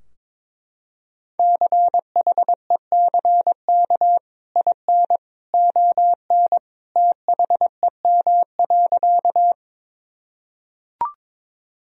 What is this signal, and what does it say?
Morse code, heard as a keyed tone, about 21 wpm, 700 Hz. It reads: CHECK IN ON THEM.
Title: Sentences from Top 1000 Words - Review Part 3 - 22wpm